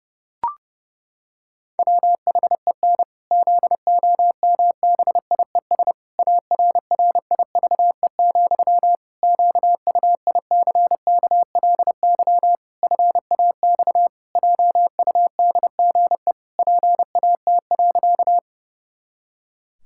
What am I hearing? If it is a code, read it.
Morse code, heard as a keyed tone, about 32 wpm, 700 Hz. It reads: WHEN ZOMBIES ARRIVE, QUICKLY FAX JUDGE PAT.
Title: Pangram Challenge v2 - 30wpm